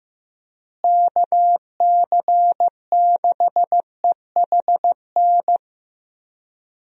Morse code, heard as a keyed tone, about 15 wpm, 700 Hz. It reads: KC6EHN
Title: USA Call Signs - 6 Characters - Repeated 3x - 15wpm